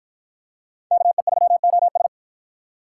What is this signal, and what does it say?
Morse code, heard as a keyed tone, about 50 wpm, 700 Hz. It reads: KE3OS